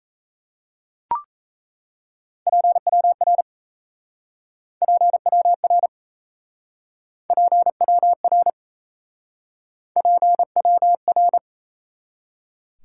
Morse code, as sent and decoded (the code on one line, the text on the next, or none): .--. .-- .-. / .--. .-- .-. / .--. .-- .-. / .--. .-- .-.
PWR PWR PWR PWR